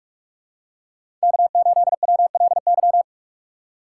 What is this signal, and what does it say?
Morse code, heard as a keyed tone, about 41 wpm, 700 Hz. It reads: K8WLY
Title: USA Call Signs - 5 Characters - 3x Character Spacing - 45wpm